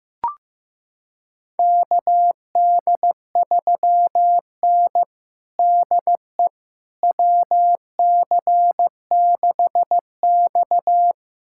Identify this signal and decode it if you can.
Morse code, heard as a keyed tone, about 15 wpm, 700 Hz. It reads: KD3N DE WC6X
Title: USA Callsign DE Callsign - 4 Characters - 15wpm